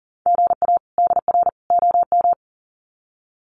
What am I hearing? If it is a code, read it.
Morse code, heard as a keyed tone, about 40 wpm, 700 Hz. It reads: GA DR OM
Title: Common QSO Phrases 40wpm